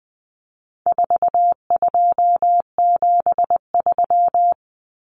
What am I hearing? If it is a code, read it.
Morse code, heard as a keyed tone, about 20 wpm, 700 Hz. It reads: 4273
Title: Numbers - 4-Digit 20wpm